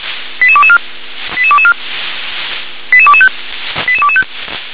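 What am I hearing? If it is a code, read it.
Five-tone selective call (ZVEI), heard as quick runs of five tones, tones of about 70 ms. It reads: 88224 88224 88225 88225
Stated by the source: ZVEI five-tone selective call received with an SDR.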